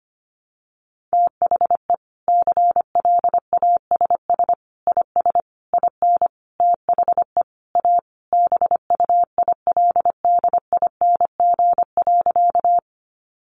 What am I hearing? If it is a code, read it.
Morse code, heard as a keyed tone, about 25 wpm, 700 Hz. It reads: THE CLASS IS IN THE A BUILDING.
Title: Additional Easy Phrases 25wpm